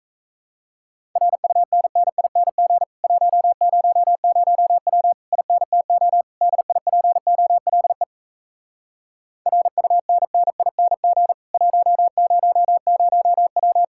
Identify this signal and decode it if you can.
Morse code, heard as a keyed tone, about 37 wpm, 700 Hz. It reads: RUNNING 100W INTO DIPOLE RUNNING 100W